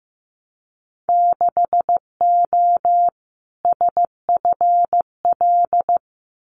Morse code, heard as a keyed tone, about 15 wpm, 700 Hz. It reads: 6O SFL
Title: Winter Field Day Exchanges Encore 15wpm